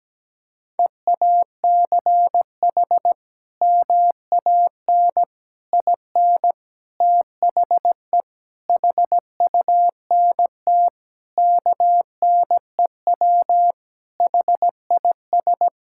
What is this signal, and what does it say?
Morse code, heard as a keyed tone, about 17 wpm, 700 Hz. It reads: EACH MAN IN THE HUNT KNEW HIS
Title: Sentences from Top 1600 Words 17wpm